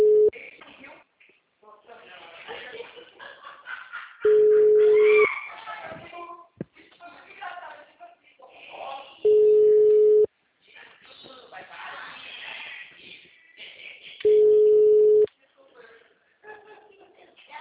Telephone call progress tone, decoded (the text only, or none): ringback tone